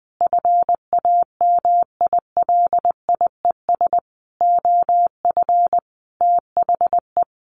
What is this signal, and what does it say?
Morse code, heard as a keyed tone, about 20 wpm, 700 Hz. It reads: FAMILIES OF THE
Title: A Sound Of Thunder 20wpm